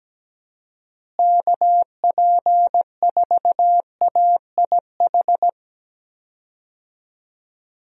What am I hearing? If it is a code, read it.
Morse code, heard as a keyed tone, about 17 wpm, 700 Hz. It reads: KP4AIH
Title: DXCC-100 Call Signs - 6 Characters - Repeated 2x - 17wpm